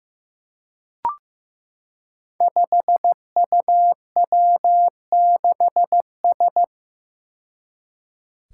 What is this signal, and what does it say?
Morse code, heard as a keyed tone, about 15 wpm, 700 Hz. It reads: 5UW6S
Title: Five Challenging Letter-Number Combinations 15wpm